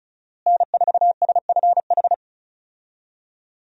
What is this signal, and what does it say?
Morse code, heard as a keyed tone, about 36 wpm, 700 Hz. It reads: N4SFH
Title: USA Call Signs - 5 Characters - Repeated 2x - 35wpm